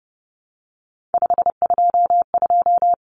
Morse code, heard as a keyed tone, about 30 wpm, 700 Hz. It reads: 522
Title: Common Cut Numbers - 3-Digit 30wpm